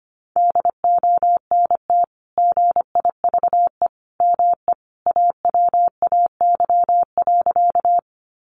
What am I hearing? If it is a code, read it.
Morse code, heard as a keyed tone, about 25 wpm, 700 Hz. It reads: DONT GIVE ME AWAY.
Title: Sentences from Top 600 Words 25wpm